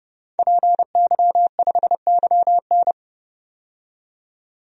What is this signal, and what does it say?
Morse code, heard as a keyed tone, about 32 wpm, 700 Hz. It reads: PY5YN
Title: DXCC-100 Call Signs - 5 Characters - 30wpm